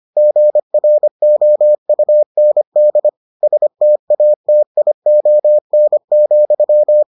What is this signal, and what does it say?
Morse code, heard as a keyed tone, about 25 wpm, 600 Hz. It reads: GROUND STATION,